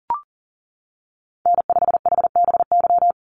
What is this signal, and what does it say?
Morse code, heard as a keyed tone, about 40 wpm, 700 Hz. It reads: N5HBY
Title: USA Call Signs 40wpm